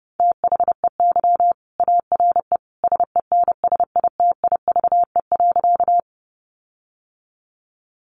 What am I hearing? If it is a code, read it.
Morse code, heard as a keyed tone, about 30 wpm, 700 Hz. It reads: THEY ARE SENSITIVE.
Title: Sentences from Top 1400 Words 30wpm